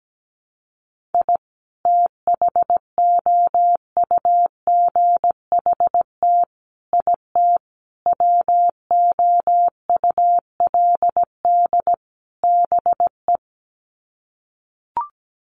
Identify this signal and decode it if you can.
Morse code, heard as a keyed tone, about 17 wpm, 700 Hz. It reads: I THOUGHT IT WOULD BE
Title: Sets of 5 Words Encore 17wpm